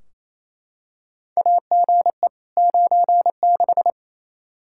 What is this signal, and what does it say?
Morse code, heard as a keyed tone, about 28 wpm, 700 Hz. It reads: AGE 96
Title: Practice QSO - Exchange 3 - More Info - 28wpm